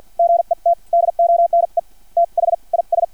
Morse code, heard as a keyed tone, about 50 wpm, spaced as for 40 wpm, 680 Hz. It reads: GET DONE THIS
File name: archived / CRSnet